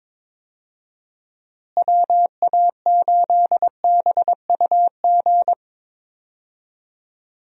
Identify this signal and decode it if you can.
Morse code, heard as a keyed tone, about 22 wpm, 700 Hz. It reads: WA8BUG
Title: USA Call Signs - 6 Characters - Repeated 2x - 22wpm